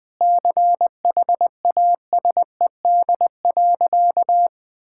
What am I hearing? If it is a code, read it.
Morse code, heard as a keyed tone, about 20 wpm, 700 Hz. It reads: CHASED.
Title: Sentences with 4-Word Phrases - 20wpm